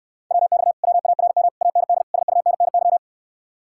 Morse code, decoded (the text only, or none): WX WARM AND SUNNY